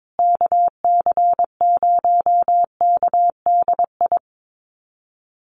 Morse code, heard as a keyed tone, about 22 wpm, 700 Hz. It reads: KC0KDI
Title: USA Call Signs Repeated 3x 22wpm